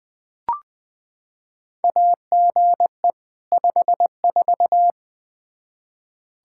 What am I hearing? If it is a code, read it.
Morse code, heard as a keyed tone, about 20 wpm, 700 Hz. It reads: AGE 54
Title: Practice QSO - Exchange 3 - More Info - 20wpm